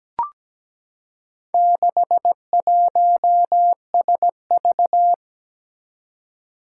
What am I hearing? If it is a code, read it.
Morse code, heard as a keyed tone, about 17 wpm, 700 Hz. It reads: 61SV
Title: Four Challenging Letter-Number Combinations 17wpm